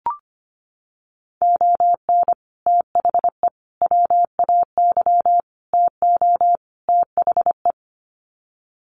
Morse code, heard as a keyed tone, about 25 wpm, 700 Hz. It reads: ON THE WAY TO THE
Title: Sets of 5 Words 25wpm